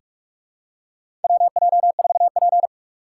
Morse code, heard as a keyed tone, about 43 wpm, 700 Hz. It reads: WJ4P
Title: USA Call Signs - 4 Characters - 4x Character Spacing - 45wpm